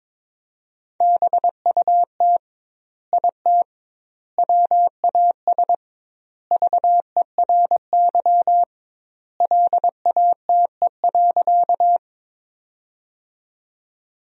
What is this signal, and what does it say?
Morse code, heard as a keyed tone, about 22 wpm, 700 Hz. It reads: BUT IT WAS VERY LATE.
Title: 2x Word-Spacing - Sentences from Top 300 Words - 22wpm